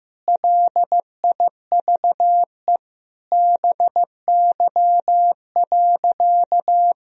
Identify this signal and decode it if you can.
Morse code, heard as a keyed tone, about 15 wpm, 700 Hz. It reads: LIVE BY.
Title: Sentences from Top 1000 Words - Review Part 3 - 15wpm